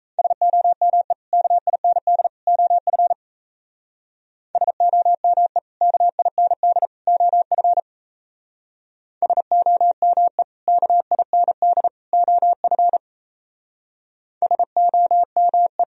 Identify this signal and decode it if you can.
Morse code, heard as a keyed tone, about 36 wpm, 700 Hz. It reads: SOME KIND OF SOME KIND OF SOME KIND OF SOME